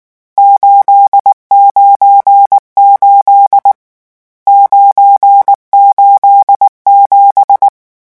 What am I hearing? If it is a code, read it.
Morse code, heard as a keyed tone, about 19 wpm, 800 Hz. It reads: 898 987
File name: morse mp3 / lecon12-vitesse 18